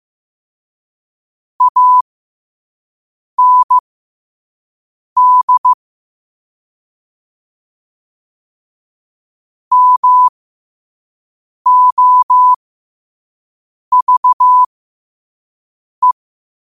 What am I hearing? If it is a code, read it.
Morse code, heard as a keyed tone, about 15 wpm, spaced as for 5 wpm, 1000 Hz. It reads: AND MOVE
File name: qotdfiles / 5wpm